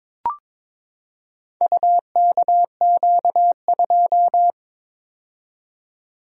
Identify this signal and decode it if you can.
Morse code, heard as a keyed tone, about 22 wpm, 700 Hz. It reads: UKQ2
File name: practice / 22wpm / Four Letter-Number Combinations - 22wpm